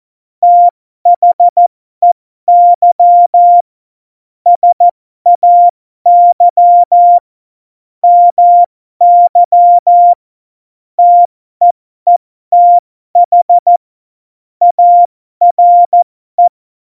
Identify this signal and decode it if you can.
Morse code, heard as a keyed tone, about 14 wpm, spaced as for 12 wpm, 700 Hz. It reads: THEY SAY MY TEETH ARE
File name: Morsekurs / Witze